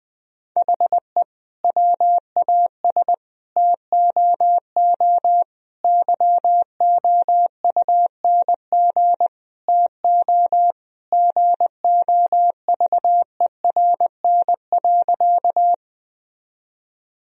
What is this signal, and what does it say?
Morse code, heard as a keyed tone, about 21 wpm, 700 Hz. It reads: HE WAS TOO YOUNG TO GOVERN.
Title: Sentences from Top 1000 Words - Review Part 2 - 20wpm